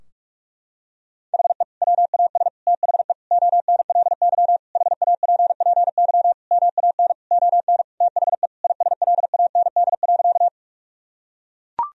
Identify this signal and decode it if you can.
Morse code, heard as a keyed tone, about 41 wpm, 700 Hz. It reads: HE WAS THE ONLY HAPPY MAN ON THE ISLAND.
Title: Sentences from Top 1000 Words - Review Part 3 - 45wpm